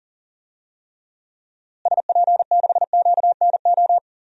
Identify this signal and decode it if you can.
Morse code, heard as a keyed tone, about 40 wpm, 700 Hz. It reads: SP6QNO
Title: DXCC-100 Call Signs - 6 Characters - Repeated 3x - 40wpm